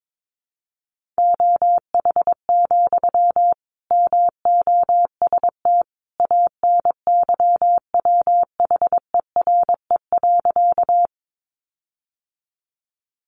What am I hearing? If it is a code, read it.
Morse code, heard as a keyed tone, about 22 wpm, 700 Hz. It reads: OH, MOST ANYWHERE.
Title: Sentences from Top 1400 Words 22wpm